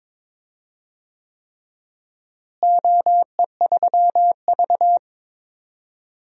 Morse code, heard as a keyed tone, about 22 wpm, 700 Hz. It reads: OE3V
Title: DXCC-100 Call Signs - 4 Characters - Repeated 3x - 22wpm